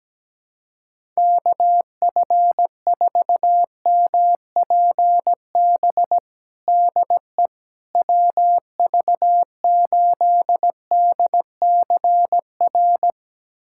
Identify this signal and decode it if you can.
Morse code, heard as a keyed tone, about 17 wpm, 700 Hz. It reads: KF4MPB DE WV8DCR
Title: USA Callsign DE Callsign 17wpm